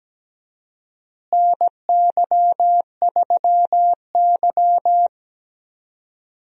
Morse code, heard as a keyed tone, about 17 wpm, 700 Hz. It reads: NY3Y
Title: USA Call Signs - 4 Characters - Repeated 3x - 17wpm